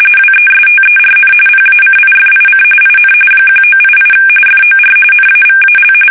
RTTY, about 45.45 baud. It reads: WESTERN PACIFIC IN 1988 FOR NAVAL
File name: fsk850 / battleshipnj